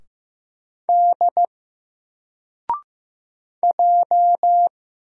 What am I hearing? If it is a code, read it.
Morse code, heard as a keyed tone, about 15 wpm, 700 Hz. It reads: D J
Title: Single Letter-Number 15wpm